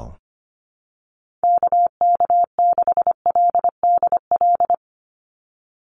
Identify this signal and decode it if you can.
Morse code, heard as a keyed tone, about 25 wpm, 700 Hz. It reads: KK6LDL
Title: USA Call Signs 25wpm